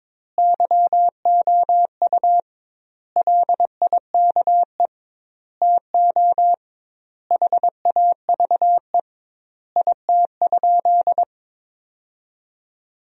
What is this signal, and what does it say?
Morse code, heard as a keyed tone, about 22 wpm, 700 Hz. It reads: YOU LIKE TO HAVE IT?
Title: 2x Word-Spacing - Sentences from Top 100 Words - 22wpm